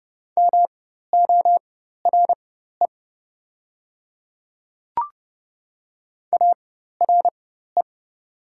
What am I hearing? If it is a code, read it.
Morse code, heard as a keyed tone, about 30 wpm, spaced as for 14 wpm, 700 Hz. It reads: MORE ARE
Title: 4x Character Spacing - Top 100 Words - 30wpm